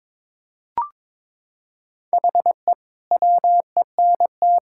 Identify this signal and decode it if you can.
Morse code, heard as a keyed tone, about 21 wpm, 700 Hz. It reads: HE WENT
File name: Sentences with 7-Word Phrases - 22wpm